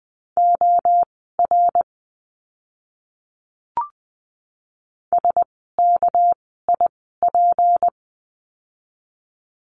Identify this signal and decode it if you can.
Morse code, heard as a keyed tone, about 20 wpm, spaced as for 14 wpm, 700 Hz. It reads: OR SKIP
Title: 199 Lesson 32f - Letter k - Words 20wpm